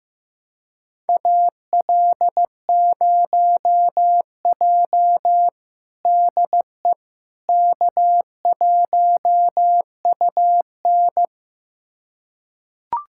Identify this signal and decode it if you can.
Morse code, heard as a keyed tone, about 15 wpm, 700 Hz. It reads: AL0J DE K1UN
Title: USA Callsign DE Callsign - 4 Characters - 15wpm